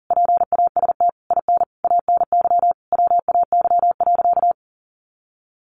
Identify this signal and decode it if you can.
Morse code, heard as a keyed tone, about 40 wpm, 700 Hz. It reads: PAST IN ANY WAY.
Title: A Sound Of Thunder 40wpm